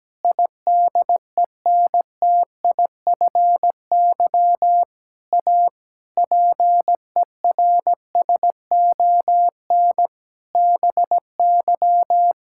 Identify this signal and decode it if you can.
Morse code, heard as a keyed tone, about 17 wpm, 700 Hz. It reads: IDENTIFY A PERSON BY
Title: Sentences from Top 1000 Words - Review Part 6 - 17wpm